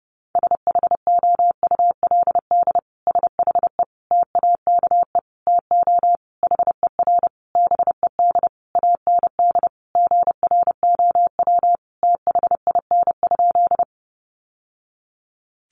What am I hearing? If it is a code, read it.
Morse code, heard as a keyed tone, about 30 wpm, 700 Hz. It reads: SHOULD SHE TAKE TO HER BED AND GROW THIN?
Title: Sentences from Top 1200 Words 30wpm